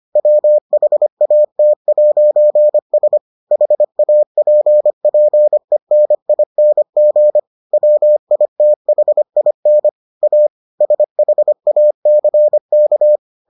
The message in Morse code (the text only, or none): WHAT'S HAPPENING WITHIN A SHACK